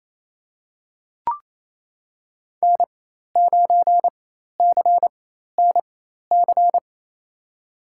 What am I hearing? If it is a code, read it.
Morse code, heard as a keyed tone, about 28 wpm, spaced as for 13 wpm, 700 Hz. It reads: N9CNC
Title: USA Call Signs - 5 Characters - 4x Character Spacing - 28wpm